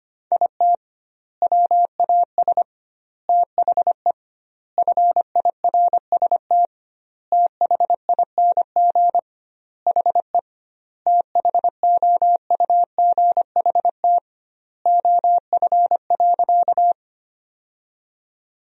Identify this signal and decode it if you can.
Morse code, heard as a keyed tone, about 25 wpm, 700 Hz. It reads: IT WAS THE FIRST THING HE THOUGHT OF.
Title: 2x Word-Spacing - Sentences from Top 300 Words - 25wpm